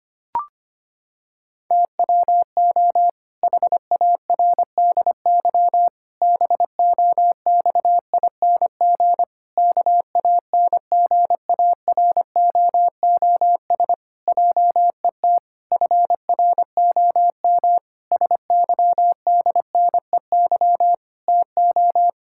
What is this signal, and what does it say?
Morse code, heard as a keyed tone, about 25 wpm, 700 Hz. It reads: TWO HARDY BOXING KANGAROOS JET FROM SYDNEY TO
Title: Pangram Challenge v2 - 25wpm